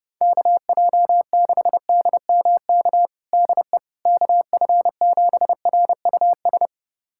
Morse code, heard as a keyed tone, about 32 wpm, 700 Hz. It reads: KJ6DMK DE KF7RUS
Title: USA Callsign DE Callsign - 6 Characters - 30wpm